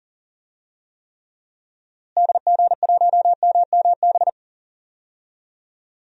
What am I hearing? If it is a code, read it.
Morse code, heard as a keyed tone, about 37 wpm, 700 Hz. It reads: DG1MMB
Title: DXCC-100 Call Signs - 6 Characters - Repeated 2x - 40wpm